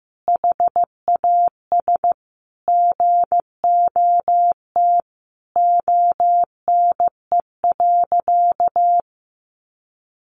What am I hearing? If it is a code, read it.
Morse code, heard as a keyed tone, about 15 wpm, 700 Hz. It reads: HAS GOT ONE.
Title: Sentences from Top 1300 Words 15wpm